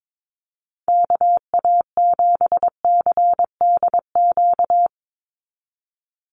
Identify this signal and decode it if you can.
Morse code, heard as a keyed tone, about 22 wpm, 700 Hz. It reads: KA7CDQ
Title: USA Call Signs 22wpm